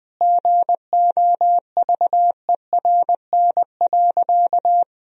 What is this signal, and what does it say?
Morse code, heard as a keyed tone, about 20 wpm, 700 Hz. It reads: GOVERN.
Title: Sentences from Top 1000 Words - Review Part 2 - 20wpm